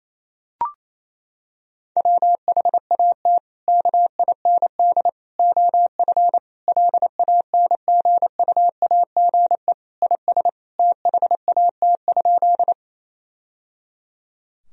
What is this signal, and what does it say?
Morse code, heard as a keyed tone, about 28 wpm, 700 Hz. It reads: WHAT KIND OF LANGUAGE IS THAT?
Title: Sentences from Top 1000 Words - Review Part 3 - 28wpm